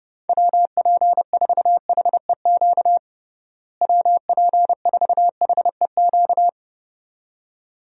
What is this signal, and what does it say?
Morse code, heard as a keyed tone, about 30 wpm, 700 Hz. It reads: WP4HEQ WP4HEQ
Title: USA Call Signs - 6 Characters - Repeated 2x - 30wpm